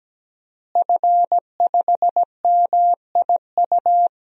Morse code, heard as a keyed tone, about 17 wpm, 700 Hz. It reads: F5MIU
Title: DXCC-100 Call Signs - 5 Characters - Repeated 2x - 17wpm